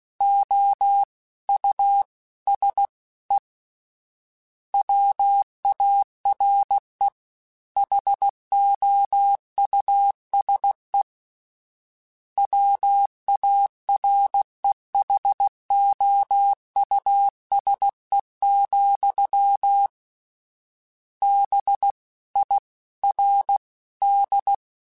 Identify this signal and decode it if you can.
Morse code, heard as a keyed tone, about 16 wpm, 800 Hz. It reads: O U S E WARE HOUSE WAREHOUSE, B I R D
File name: General / General-mp3s / Houses @15